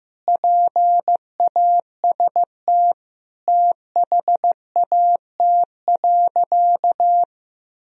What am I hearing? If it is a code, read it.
Morse code, heard as a keyed tone, about 16 wpm, 700 Hz. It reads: PAST THAT.
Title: Sentences from Top 1000 Words - Review Part 3 - 15wpm